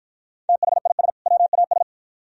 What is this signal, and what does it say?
Morse code, heard as a keyed tone, about 51 wpm, 700 Hz. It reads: THIS WAS